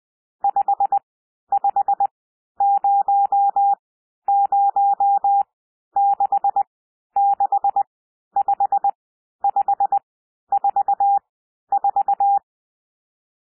Morse code, heard as a keyed tone, about 20 wpm, spaced as for 11 wpm, 800 Hz. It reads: 5500665544
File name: SOUND NUMBERS 20 wpm